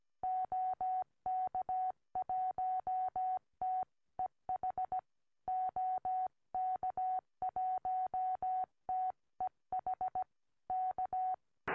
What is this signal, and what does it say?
Morse code, heard as a keyed tone, about 17 wpm, 740 Hz. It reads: OK1TEH OK1TEH K
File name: au / gm4vvx 2m